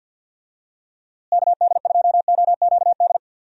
Morse code, heard as a keyed tone, about 51 wpm, 700 Hz. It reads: KD2OQD
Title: USA Call Signs - 6 Characters - Repeated 3x - 50wpm